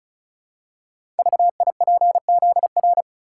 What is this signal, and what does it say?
Morse code, heard as a keyed tone, about 36 wpm, 690 Hz. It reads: VIPZR
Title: Code Groups - 5 Letters - 35wpm